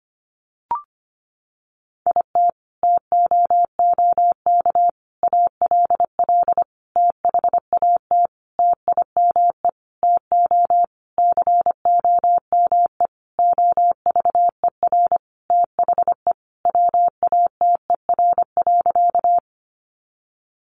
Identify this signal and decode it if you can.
Morse code, heard as a keyed tone, about 25 wpm, 700 Hz. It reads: IT TOOK ALL THAT TIME TO COME OVER THE WATER.
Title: Sentences from Top 300 Words 25wpm